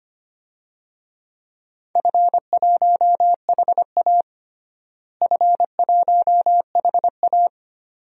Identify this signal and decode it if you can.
Morse code, heard as a keyed tone, about 25 wpm, 700 Hz. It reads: F1HA F1HA
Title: DXCC-100 Call Signs - 4 Characters - Repeated 2x - 25wpm